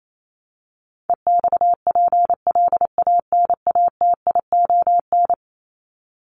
Morse code, heard as a keyed tone, about 28 wpm, 700 Hz. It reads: EXPLANATION
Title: Top 1751-2000 Words 28wpm